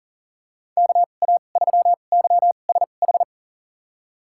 Morse code, heard as a keyed tone, about 40 wpm, spaced as for 29 wpm, 700 Hz. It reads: KA3YSH
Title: USA Call Signs - 2x Character Spacing - 40wpm